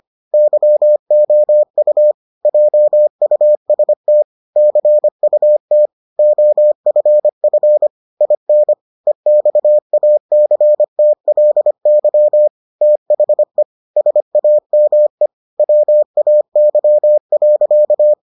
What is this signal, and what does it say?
Morse code, heard as a keyed tone, about 25 wpm, 600 Hz. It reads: YOU JUST CUT OFF IN EXACTLY THE SAME WAY.